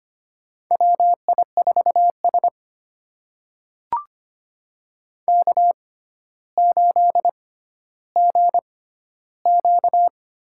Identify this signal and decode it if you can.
Morse code, heard as a keyed tone, about 25 wpm, spaced as for 13 wpm, 700 Hz. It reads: WI4S K8GQ
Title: USA Call Signs - 4 Characters - 6x Character Spacing - 25wpm